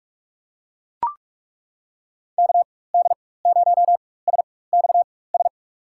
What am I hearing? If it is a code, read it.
Morse code, heard as a keyed tone, about 41 wpm, spaced as for 20 wpm, 700 Hz. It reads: KD0SXS
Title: USA Call Signs - 6 Characters - 4x Character Spacing - 45wpm